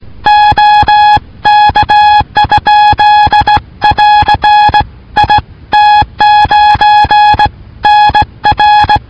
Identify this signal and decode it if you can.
Morse code, heard as a keyed tone, about 18 wpm, 840 Hz. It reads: OK? <AR> IT9NR